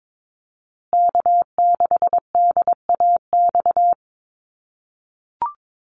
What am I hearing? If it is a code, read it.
Morse code, heard as a keyed tone, about 22 wpm, 700 Hz. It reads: K6DAX
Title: USA Call Signs Repeated 3x 22wpm